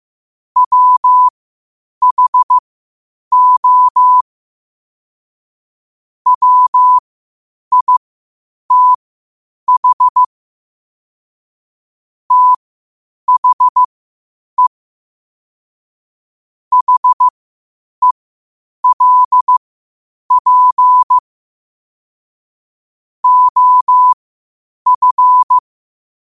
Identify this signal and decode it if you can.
Morse code, heard as a keyed tone, about 15 wpm, spaced as for 8 wpm, 1000 Hz. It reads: WHO WITH THE HELP OF
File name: qotdfiles / 8wpm